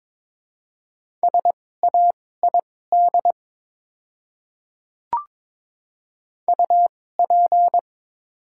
Morse code, heard as a keyed tone, about 22 wpm, spaced as for 16 wpm, 700 Hz. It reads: SAID UP
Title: 2x Character Spacing - Top 100 Words - 22wpm